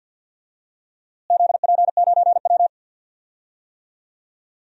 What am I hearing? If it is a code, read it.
Morse code, heard as a keyed tone, about 52 wpm, 700 Hz. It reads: ZP9W
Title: DXCC-100 Call Signs - 4 Characters - Repeated 3x - 50wpm